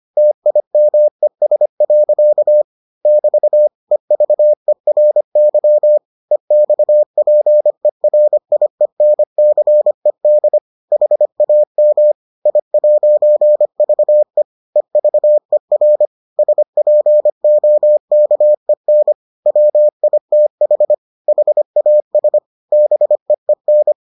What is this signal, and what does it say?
Morse code, heard as a keyed tone, about 25 wpm, 600 Hz. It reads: TIMES. <BT> EVERY EXPERIENCED HAM I'VE EVER SPOKEN WITH HAS BEEN